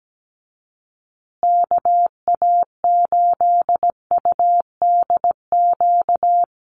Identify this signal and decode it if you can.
Morse code, heard as a keyed tone, about 17 wpm, 700 Hz. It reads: KA8UDQ
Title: USA Call Signs Repeated 3x 17wpm